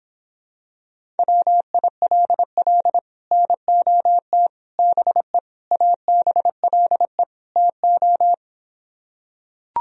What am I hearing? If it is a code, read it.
Morse code, heard as a keyed tone, about 26 wpm, 700 Hz. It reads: WILL NOT BE ABLE TO